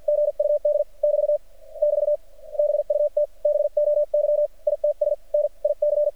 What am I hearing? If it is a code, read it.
Morse code, heard as a keyed tone, about 50 wpm, spaced as for 42 wpm, 590 Hz. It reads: GUD <BT> <BT> BUT BOY ITS NIC
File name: archived / CRSnet